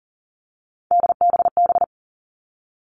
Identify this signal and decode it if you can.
Morse code, heard as a keyed tone, about 39 wpm, 700 Hz. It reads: DBB
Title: Three Challenging Letter-Number Combinations 40wpm